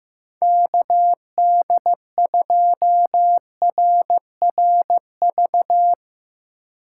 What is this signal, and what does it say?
Morse code, heard as a keyed tone, about 15 wpm, 700 Hz. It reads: KD2RRV
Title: USA Call Signs - 6 Characters - 3x Character Spacing - 15wpm